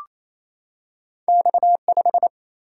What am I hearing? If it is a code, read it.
Morse code, heard as a keyed tone, about 28 wpm, 700 Hz. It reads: X5
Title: Single-Letter Single-Number 28wpm